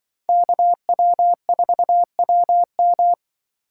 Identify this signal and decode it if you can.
Morse code, heard as a keyed tone, about 25 wpm, 700 Hz. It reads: KW4WM
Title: Speed Racing - USA Call Signs 20wpm